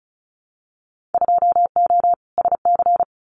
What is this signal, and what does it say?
Morse code, heard as a keyed tone, about 34 wpm, 700 Hz. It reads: 2O SC
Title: Winter Field Day Exchanges 35wpm